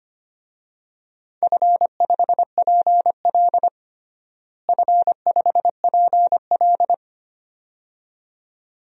Morse code, heard as a keyed tone, about 25 wpm, 700 Hz. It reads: F5PL F5PL
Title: DXCC-100 Call Signs - 4 Characters - Repeated 2x - 25wpm